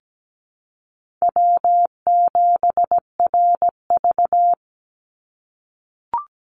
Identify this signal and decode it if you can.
Morse code, heard as a keyed tone, about 17 wpm, 700 Hz. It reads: W7RV
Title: NA SOTA Chaser Callsigns 17wpm